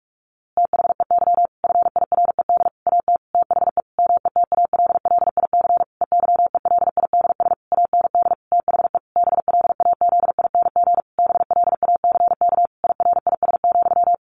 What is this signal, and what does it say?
Morse code, heard as a keyed tone, about 44 wpm, 700 Hz. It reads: THEY FIRED AT THE METALLIC EYELIDS AND THE BLAZING BLACK IRIS,
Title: A Sound Of Thunder 45wpm